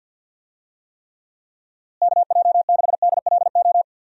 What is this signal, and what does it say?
Morse code, heard as a keyed tone, about 50 wpm, 700 Hz. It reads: KJ6DLO
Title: USA Call Signs - 6 Characters - Repeated 3x - 50wpm